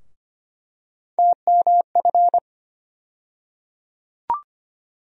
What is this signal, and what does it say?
Morse code, heard as a keyed tone, about 25 wpm, 700 Hz. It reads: TMF
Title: USA Call Sign Postfixes - 3 Letters 25wpm